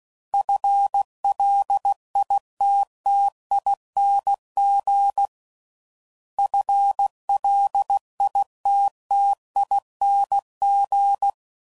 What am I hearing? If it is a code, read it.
Morse code, heard as a keyed tone, about 16 wpm, 800 Hz. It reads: FLITTING FLITTING